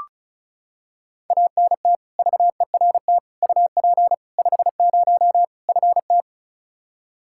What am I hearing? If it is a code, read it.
Morse code, heard as a keyed tone, about 36 wpm, 700 Hz. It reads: ANT VERT UP 50 FT
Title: Practice QSO - Exchange 2 - Info - 35wpm